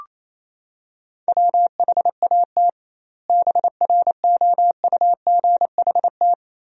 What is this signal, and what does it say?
Morse code, heard as a keyed tone, about 28 wpm, 700 Hz. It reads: WHAT BROUGHT
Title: 2x Word-Spacing - Sentences from Top 500 Words - 28wpm